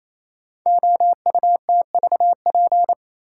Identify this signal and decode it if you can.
Morse code, heard as a keyed tone, about 27 wpm, 700 Hz. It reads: OUTVP
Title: Code Groups - 5 Letter:Numbers - 28wpm